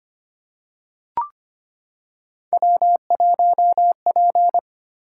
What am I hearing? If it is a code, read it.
Morse code, heard as a keyed tone, about 25 wpm, 700 Hz. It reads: W1P
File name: Three Letter-Number Combinations - 25wpm